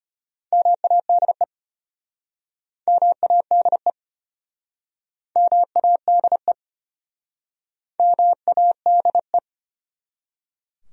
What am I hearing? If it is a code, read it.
Morse code, heard as a keyed tone, about 33 wpm, 700 Hz. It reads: MADE MADE MADE MADE